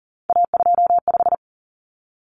Morse code, heard as a keyed tone, about 40 wpm, 700 Hz. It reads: A25
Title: Common Cut Numbers - 3-Digit 40wpm